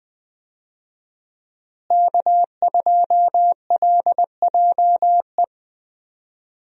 Morse code, heard as a keyed tone, about 20 wpm, 700 Hz. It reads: K2LJE
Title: USA Call Signs - 5 Characters - Repeated 3x - 20wpm